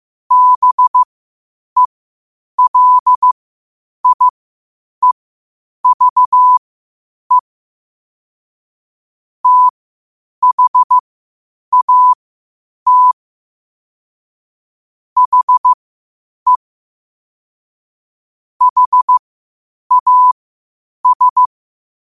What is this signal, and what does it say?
Morse code, heard as a keyed tone, about 15 wpm, spaced as for 8 wpm, 1000 Hz. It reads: BELIEVE THAT HE HAS
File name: qotdfiles / 8wpm